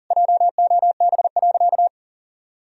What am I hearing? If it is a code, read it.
Morse code, heard as a keyed tone, about 37 wpm, 700 Hz. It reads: JOB.
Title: Sentences from Top 1000 Words - Review Part 3 - 40wpm